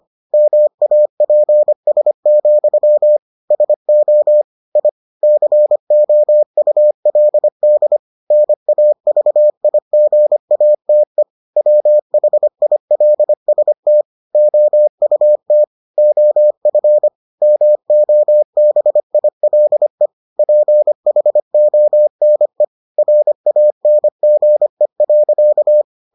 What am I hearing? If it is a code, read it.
Morse code, heard as a keyed tone, about 25 wpm, 600 Hz. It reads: MAPS, SO I COULD NAVIGATE WHILST OUT OF MOBILE PHONE RANGE.